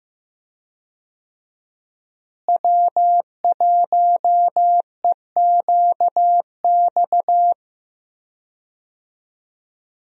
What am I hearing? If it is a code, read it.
Morse code, heard as a keyed tone, about 15 wpm, 700 Hz. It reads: W1EQX
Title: USA Call Signs - 5 Characters - Repeated 3x - 15wpm